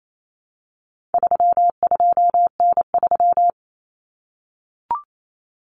Morse code, heard as a keyed tone, about 28 wpm, 700 Hz. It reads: 32N3
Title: Common Cut Numbers - 4-Digit 28wpm